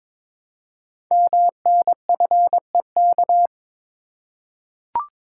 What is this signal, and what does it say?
Morse code, heard as a keyed tone, about 22 wpm, 700 Hz. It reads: MNFEK
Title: Code Groups - 5 Letters - 22wpm